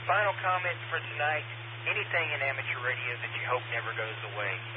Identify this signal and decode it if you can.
Emergency Alert System (SAME) tones, heard as harsh buzzing bursts, with speech. It is heard from 0.7 to 4.0 s.